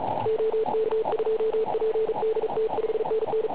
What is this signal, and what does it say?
Morse code, heard as a keyed tone, about 35 wpm, 440 Hz, with radio noise. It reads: OK2PDT 5NN
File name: Pile UP CUC204